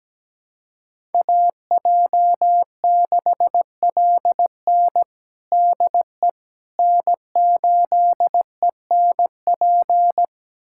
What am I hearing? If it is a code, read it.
Morse code, heard as a keyed tone, about 17 wpm, 700 Hz. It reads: AJ6LN DE N8ENP
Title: USA Callsign DE Callsign - 5 Characters - 17wpm